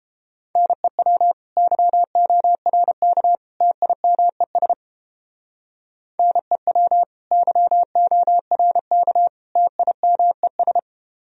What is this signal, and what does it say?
Morse code, heard as a keyed tone, about 29 wpm, 700 Hz. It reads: NEW YORK TIMES NEW YORK TIMES